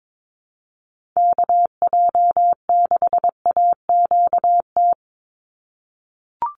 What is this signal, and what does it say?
Morse code, heard as a keyed tone, about 22 wpm, 700 Hz. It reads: KJ6AQT
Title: USA Call Signs 22wpm